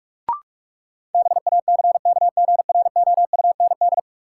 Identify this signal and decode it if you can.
Morse code, heard as a keyed tone, about 47 wpm, 700 Hz. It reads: BACKGROUND
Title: Sentences from Top 1300 Words - New Word Drill Alt - 45wpm